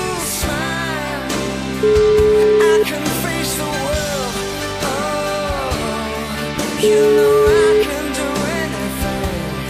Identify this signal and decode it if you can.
Ringback tone (Europe, 425 Hz), with noise.